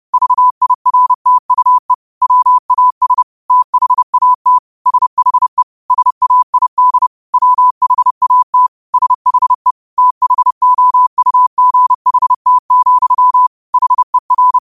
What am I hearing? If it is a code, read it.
Morse code, heard as a keyed tone, about 30 wpm, 1000 Hz. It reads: VIRTUE WAS THAT SHE SAID WHAT SHE THOUGHT, HER
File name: qotdfiles / 30wpm